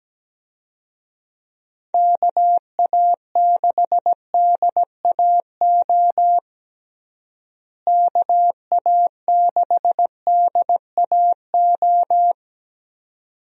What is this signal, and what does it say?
Morse code, heard as a keyed tone, about 17 wpm, 700 Hz. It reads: KA6DAO KA6DAO
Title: USA Call Signs - 6 Characters - Repeated 2x - 17wpm